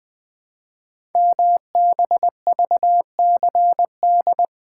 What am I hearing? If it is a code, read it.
Morse code, heard as a keyed tone, about 20 wpm, 700 Hz. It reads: MBVCD
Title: Code Groups - 5 Letters - 20wpm